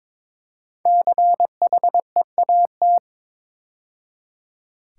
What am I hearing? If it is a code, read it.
Morse code, heard as a keyed tone, about 22 wpm, 700 Hz. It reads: CHEAT
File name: 5 Letter Words - First 3 Letters Given - 22wpm